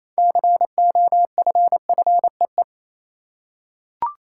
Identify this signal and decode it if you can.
Morse code, heard as a keyed tone, about 28 wpm, 700 Hz. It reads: COFFEE
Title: Six Letter Words - 28wpm